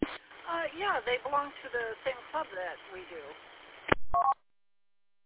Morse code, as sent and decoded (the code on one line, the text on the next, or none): none